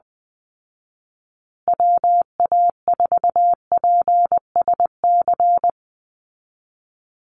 Morse code, heard as a keyed tone, about 20 wpm, 700 Hz. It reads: WA4PSC
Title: Call Signs Repeated 2x - 20wpm